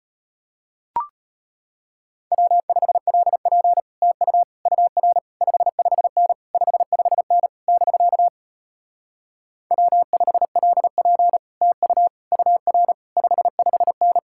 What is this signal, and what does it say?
Morse code, heard as a keyed tone, about 33 wpm, 700 Hz. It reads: W5LP TU UR 55N 55N <BK> W5LP TU UR 55N